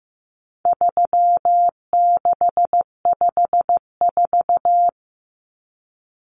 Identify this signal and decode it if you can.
Morse code, heard as a keyed tone, about 15 wpm, 700 Hz. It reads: 3654
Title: Common Cut Numbers - 4-Digit 15wpm